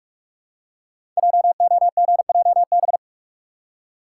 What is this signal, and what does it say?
Morse code, heard as a keyed tone, about 44 wpm, 690 Hz. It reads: JOGJB
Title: Code Groups - 5 Letters - 45wpm